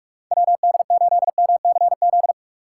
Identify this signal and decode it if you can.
Morse code, heard as a keyed tone, about 45 wpm, 700 Hz. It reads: WD8MCZ
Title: USA Call Signs - 6 Characters - 3x Character Spacing - 45wpm